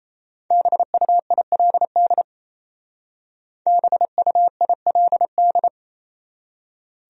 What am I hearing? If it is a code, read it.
Morse code, heard as a keyed tone, about 36 wpm, 700 Hz. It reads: BUILD BUILD